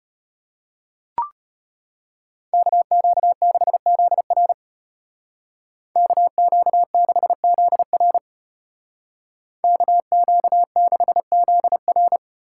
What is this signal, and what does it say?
Morse code, heard as a keyed tone, about 34 wpm, 700 Hz. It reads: KQ6ZR KQ6ZR KQ6ZR